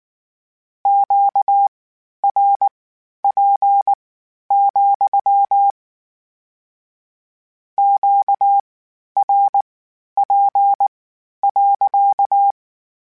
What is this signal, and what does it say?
Morse code, heard as a keyed tone, about 19 wpm, spaced as for 11 wpm, 800 Hz. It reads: QRP, QRP.